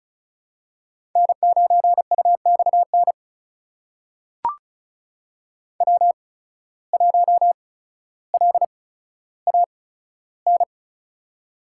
Morse code, heard as a keyed tone, about 36 wpm, spaced as for 15 wpm, 700 Hz. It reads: N9UXN W1LAN
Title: USA Call Signs - 5 Characters - 8x Character Spacing - 35wpm